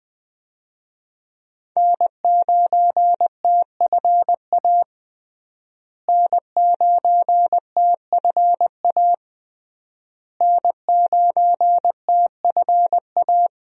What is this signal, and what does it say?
Morse code, heard as a keyed tone, about 21 wpm, 700 Hz. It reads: N9TFA N9TFA N9TFA
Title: USA Call Signs - 5 Characters - Repeated 3x - 20wpm